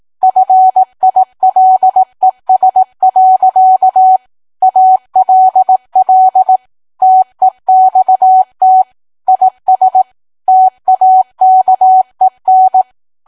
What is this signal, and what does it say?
Morse code, heard as a keyed tone, about 18 wpm, 750 Hz. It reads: FILES. ALL TEXT IS TAKEN